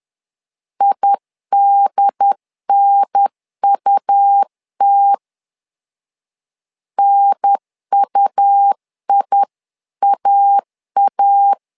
Morse code, heard as a keyed tone, about 10 wpm, 800 Hz. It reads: IDNUT NUIAA
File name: LEZIONI / 2 UHDN / 2 UHDN 10wpm 1904